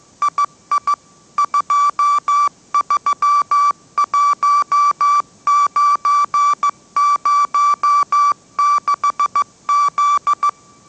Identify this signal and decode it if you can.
Morse code, heard as a keyed tone, about 15 wpm, 1160 Hz. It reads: II 231906Z
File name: mrx / mrx066 16wpm